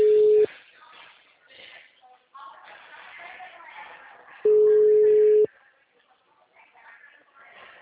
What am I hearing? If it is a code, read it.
Ringback tone (Europe, 425 Hz), with speech.